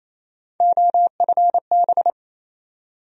Morse code, heard as a keyed tone, about 28 wpm, 700 Hz. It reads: OFB
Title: USA Call Sign Postfixes - 3 Letters 28wpm